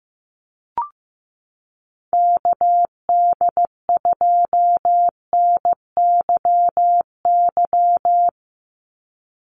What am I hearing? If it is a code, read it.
Morse code, heard as a keyed tone, about 15 wpm, 700 Hz. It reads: KD2NYY
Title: USA Call Signs 15wpm